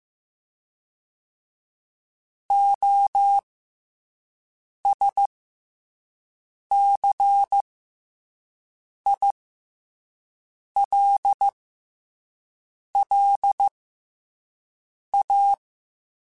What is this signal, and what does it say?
Morse code, heard as a keyed tone, about 15 wpm, spaced as for 5 wpm, 800 Hz. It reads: OSCILLA
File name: Special INGs-mp3s / Special INGs-mp3s / SlowToFaster ing @10